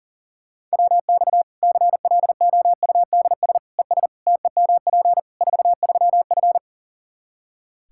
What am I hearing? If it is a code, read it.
Morse code, heard as a keyed tone, about 40 wpm, 700 Hz. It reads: WX CLOUDS ES TEMP 43F
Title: Practice QSO - Exchange 2 - Info - 40wpm